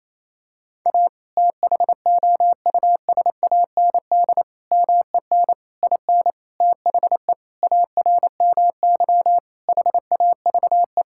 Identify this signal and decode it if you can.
Morse code, heard as a keyed tone, about 27 wpm, 690 Hz. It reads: A THOUSAND MEN IN THE ARMY HAVE
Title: Sentences from Top 1000 Words - Review Part 6 - 28wpm